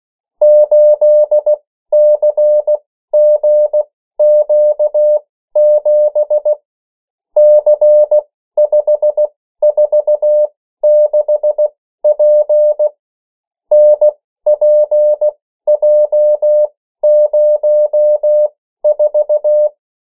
Morse code, heard as a keyed tone, about 16 wpm, spaced as for 13 wpm, 600 Hz. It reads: 8CGQ7 C546P NPJ04
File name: morse mp3 / 14 urok 13